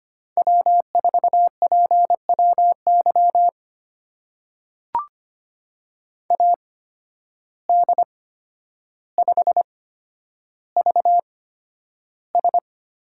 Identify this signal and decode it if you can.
Morse code, heard as a keyed tone, about 25 wpm, spaced as for 11 wpm, 700 Hz. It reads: W4PWY AD5VS
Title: USA Call Signs - 5 Characters - 8x Character Spacing - 25wpm